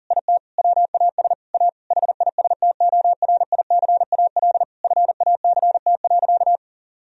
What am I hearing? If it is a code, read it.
Morse code, heard as a keyed tone, about 40 wpm, 700 Hz. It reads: IT WAS A HISTORICAL FACT.
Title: Sentences from Top 1000 Words - Review Part 4 - 40wpm